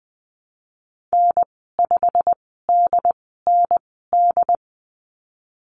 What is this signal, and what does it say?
Morse code, heard as a keyed tone, about 19 wpm, spaced as for 14 wpm, 700 Hz. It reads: N5DND
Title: 147 Lesson 24g - Letter y - Call Signs 20wpm